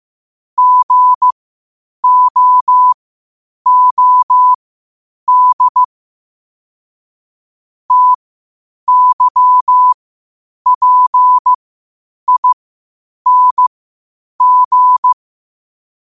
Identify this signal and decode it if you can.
Morse code, heard as a keyed tone, about 15 wpm, spaced as for 8 wpm, 1000 Hz. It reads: GOOD TYPING
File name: qotdfiles / 8wpm